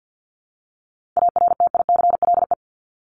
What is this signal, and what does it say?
Morse code, heard as a keyed tone, about 50 wpm, 700 Hz. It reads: ARTICLE
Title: Top 751-1000 Words 50wpm